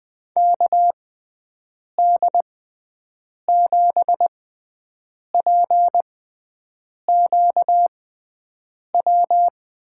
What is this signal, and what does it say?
Morse code, heard as a keyed tone, about 20 wpm, spaced as for 7 wpm, 700 Hz. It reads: KD7PQW
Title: USA Call Signs - 6x Character Spacing - 20wpm